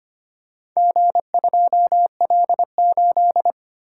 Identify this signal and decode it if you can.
Morse code, heard as a keyed tone, about 25 wpm, 700 Hz. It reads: G2L8
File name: practice / 25wpm / Four Letter-Number Combinations - 25wpm